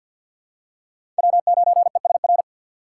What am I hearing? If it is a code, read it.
Morse code, heard as a keyed tone, about 47 wpm, 700 Hz. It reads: W 9 E S R
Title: USA Call Signs - 6x Character Spacing - 50wpm